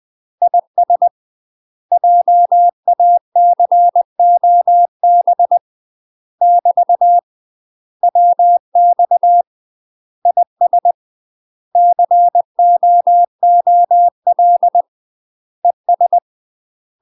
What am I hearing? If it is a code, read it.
Morse code, heard as a keyed tone, about 20 wpm, 700 Hz. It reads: IS JACOB <BT> WX IS COOL ES
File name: UK MASTER QSOs TRAINING / 20WPM / UK MASTER QSOs 0067